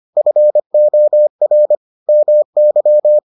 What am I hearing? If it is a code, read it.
Morse code, heard as a keyed tone, about 25 wpm, 600 Hz. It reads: FOR MY